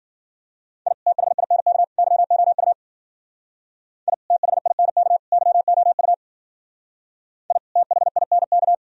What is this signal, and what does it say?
Morse code, heard as a keyed tone, about 51 wpm, 700 Hz. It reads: I THINK YOU I THINK YOU I THINK